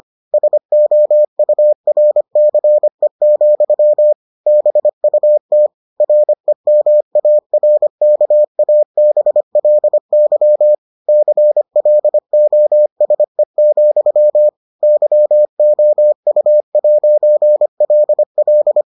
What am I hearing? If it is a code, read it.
Morse code, heard as a keyed tone, about 25 wpm, 600 Hz. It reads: SOURCE, BUT REMARKABLY CLOSE, YOU'LL